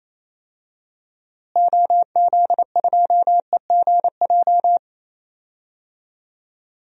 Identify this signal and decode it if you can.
Morse code, heard as a keyed tone, about 28 wpm, 700 Hz. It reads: OZ2EGJ
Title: DXCC-100 Call Signs - 6 Characters - Repeated 2x - 28wpm